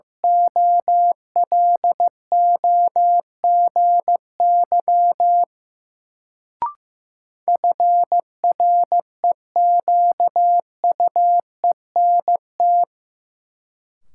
Morse code, heard as a keyed tone, about 15 wpm, 700 Hz. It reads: OLOGY FREQUENT
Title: Sentences from Top 1400 Words - New Word Drill Alt - 15wpm